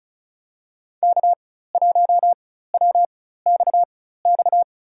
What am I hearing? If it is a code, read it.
Morse code, heard as a keyed tone, about 36 wpm, spaced as for 16 wpm, 700 Hz. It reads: K1WXX
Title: USA Call Signs - 5 Characters - 4x Character Spacing - 35wpm